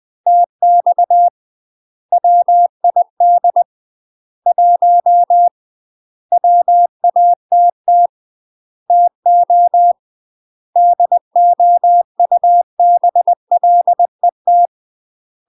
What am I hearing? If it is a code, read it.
Morse code, heard as a keyed tone, about 20 wpm, 700 Hz. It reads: TX WID 1 WATT TO DOUBLET
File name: UK MASTER QSOs TRAINING / 20WPM / UK MASTER QSOs 0037